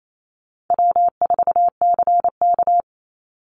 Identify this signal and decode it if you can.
Morse code, heard as a keyed tone, about 28 wpm, 700 Hz. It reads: W4CK
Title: Easy Call Signs - 28wpm